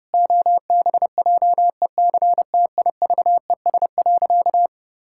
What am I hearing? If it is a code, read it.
Morse code, heard as a keyed tone, about 32 wpm, 700 Hz. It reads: OBJECTIVES.
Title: Pangram Challenge v2 - 30wpm